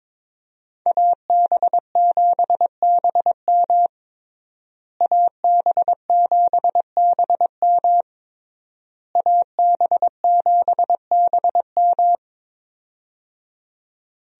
Morse code, heard as a keyed tone, about 22 wpm, 700 Hz. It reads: AB7BM AB7BM AB7BM
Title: USA Call Signs - 5 Characters - Repeated 3x - 22wpm